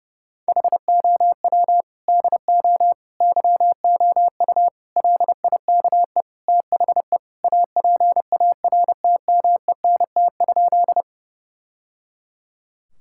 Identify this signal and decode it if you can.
Morse code, heard as a keyed tone, about 30 wpm, 700 Hz. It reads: HOW DO YOU LIKE THE APARTMENT?
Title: Sentences from Top 1000 Words - Review Part 6 - 30wpm